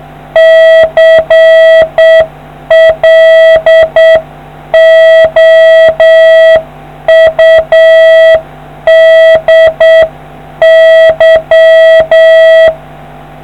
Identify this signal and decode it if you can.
Morse code, heard as a keyed tone, about 8 wpm, 650 Hz. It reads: CLOUDY